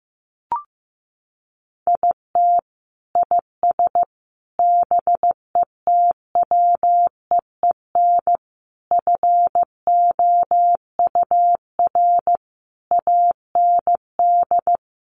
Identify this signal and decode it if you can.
Morse code, heard as a keyed tone, about 15 wpm, 700 Hz. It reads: IT IS BETWEEN FOUR AND
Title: Sentences from Top 1500 Words 15wpm